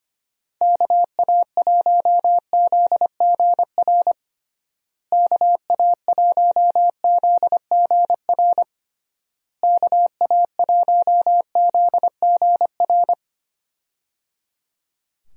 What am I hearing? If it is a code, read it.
Morse code, heard as a keyed tone, about 25 wpm, 700 Hz. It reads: KA1ZGR KA1ZGR KA1ZGR
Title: USA Call Signs - 6 Characters - Repeated 3x - 25wpm